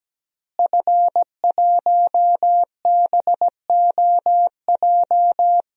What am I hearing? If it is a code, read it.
Morse code, heard as a keyed tone, about 17 wpm, 700 Hz. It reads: F1BOJ
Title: DXCC-100 Call Signs - 5 Characters - Repeated 3x - 17wpm